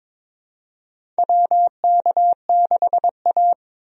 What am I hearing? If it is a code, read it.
Morse code, heard as a keyed tone, about 21 wpm, 700 Hz. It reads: WK6A
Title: USA Call Signs - 4 Characters - 22wpm